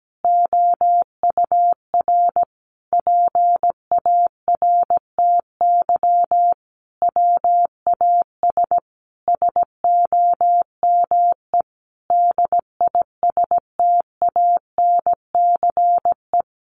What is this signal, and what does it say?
Morse code, heard as a keyed tone, about 17 wpm, 700 Hz. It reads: OUR PARTY WAS SOME DISTANCE
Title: Sentences from Top 1400 Words 17wpm